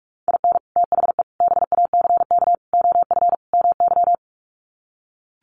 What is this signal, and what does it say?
Morse code, heard as a keyed tone, about 44 wpm, 700 Hz. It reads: IN THE BACK OF MY
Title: Sets of 5 Words Encore 45wpm